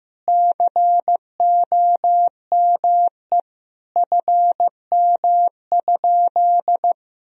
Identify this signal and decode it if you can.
Morse code, heard as a keyed tone, about 15 wpm, 700 Hz. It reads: COME FM?
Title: Sentences from Top 200 Words and Abbreviations 15wpm